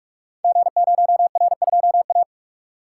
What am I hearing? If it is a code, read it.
Morse code, heard as a keyed tone, about 48 wpm, 700 Hz. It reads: G0R2A
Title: Code Groups - 5 Letter:Numbers - 45wpm